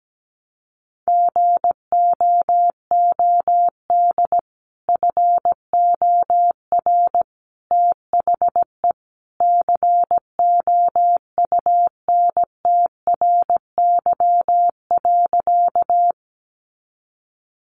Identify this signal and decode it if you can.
Morse code, heard as a keyed tone, about 17 wpm, 700 Hz. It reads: GOOD FOR THE COUNTRY.
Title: Sentences from Top 300 Words 17wpm